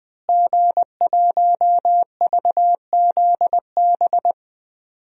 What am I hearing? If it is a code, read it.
Morse code, heard as a keyed tone, about 20 wpm, 700 Hz. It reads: G1VZB
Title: DXCC-100 Call Signs - 5 Characters - Repeated 3x - 20wpm